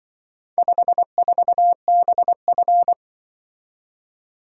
Morse code, heard as a keyed tone, about 25 wpm, 700 Hz. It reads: 54BF
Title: Speed-Racing - Four Letter-Number Combinations - 20wpm